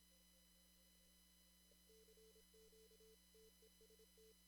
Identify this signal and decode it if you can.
Morse code, heard as a keyed tone, about 27 wpm, 440 Hz. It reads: CQ TEST
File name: CW / kt5j wpx cw-op-k5pi